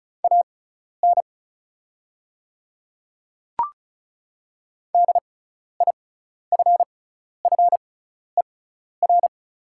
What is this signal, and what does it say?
Morse code, heard as a keyed tone, about 36 wpm, spaced as for 12 wpm, 700 Hz. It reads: AN DIFFER
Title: 6x Character Spacing - Top 200 Words - 35wpm